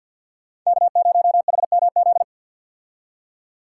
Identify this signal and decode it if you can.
Morse code, heard as a keyed tone, about 48 wpm, 690 Hz. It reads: K0HMZ
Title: K1USN SST QSO Practice - 50wpm